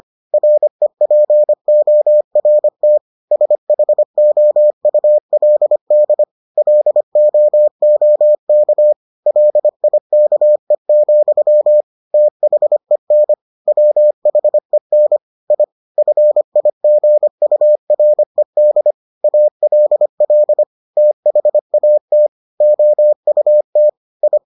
Morse code, heard as a keyed tone, about 25 wpm, 600 Hz. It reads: REPORT SHOULD LOOK LIKE, THEN WHEN I FIGURED ALL THAT OUT I